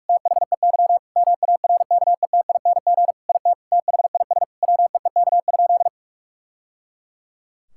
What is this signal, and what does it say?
Morse code, heard as a keyed tone, about 41 wpm, 700 Hz. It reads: THEY MARKETING IT THIS WEEK?
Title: Sentences from Top 1000 Words - Review Part 5 - 45wpm